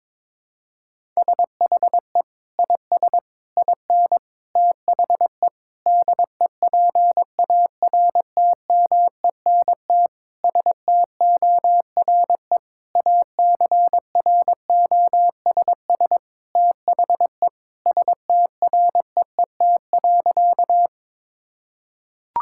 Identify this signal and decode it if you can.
Morse code, heard as a keyed tone, about 22 wpm, 700 Hz. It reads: SHE IS IN THE DEPARTMENT STORE ACROSS THE STREET.
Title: Sentences from Top 1000 Words - Review Part 5 - 22wpm